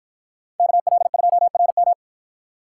Morse code, heard as a keyed tone, about 53 wpm, 700 Hz. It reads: KL2RM